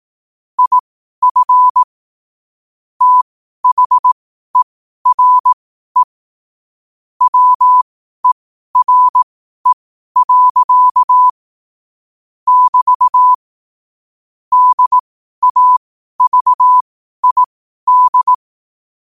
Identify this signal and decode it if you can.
Morse code, heard as a keyed tone, about 18 wpm, spaced as for 12 wpm, 1000 Hz. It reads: IF THERE WERE. <BT> DAVID
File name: qotdfiles / 12wpm